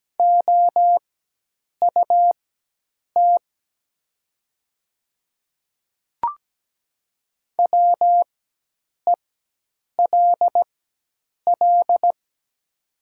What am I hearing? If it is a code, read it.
Morse code, heard as a keyed tone, about 17 wpm, spaced as for 8 wpm, 700 Hz. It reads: OUT WELL
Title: 4x Character Spacing - Top 300 Words - 17wpm